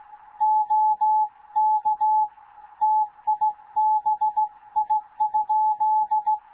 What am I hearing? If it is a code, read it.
Morse code, heard as a keyed tone, about 16 wpm, 860 Hz. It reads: OK TIBI?
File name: Morze-HA2MN / hangp / 2 11 GYAKORLAT